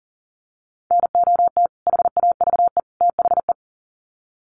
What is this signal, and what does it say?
Morse code, heard as a keyed tone, about 40 wpm, 700 Hz. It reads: NOT HAVE THE
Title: Sets of 3 Words 40wpm